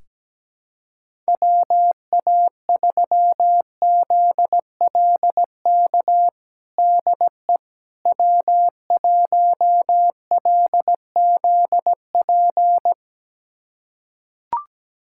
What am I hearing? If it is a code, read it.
Morse code, heard as a keyed tone, about 17 wpm, 700 Hz. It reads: WA3ZLK DE W1LZP
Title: USA Callsign DE Callsign 17wpm